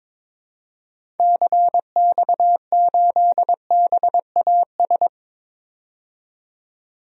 Morse code, heard as a keyed tone, about 22 wpm, 700 Hz. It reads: CX8BAS
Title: DXCC-100 Call Signs - 6 Characters - 22wpm